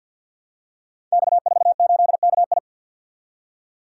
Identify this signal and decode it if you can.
Morse code, heard as a keyed tone, about 46 wpm, 700 Hz. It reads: X48KI
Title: Code Groups - 5 Letter:Numbers - 50wpm